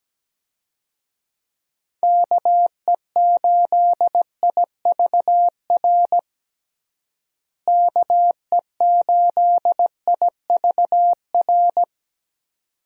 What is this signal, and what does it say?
Morse code, heard as a keyed tone, about 17 wpm, 700 Hz. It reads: KE8IVR KE8IVR
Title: USA Call Signs - 6 Characters - Repeated 2x - 17wpm